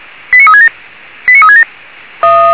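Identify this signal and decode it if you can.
Five-tone selective call (ZVEI), heard as quick runs of five tones, tones of about 80 ms. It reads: 89367 89367
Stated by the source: ZVEI five-tone selective call 89365 received with an SDR.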